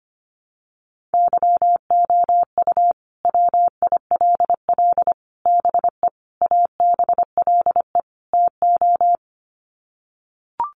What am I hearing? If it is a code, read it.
Morse code, heard as a keyed tone, about 25 wpm, 700 Hz. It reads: YOU WILL BE ABLE TO
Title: Sets of 5 Words Encore2 25wpm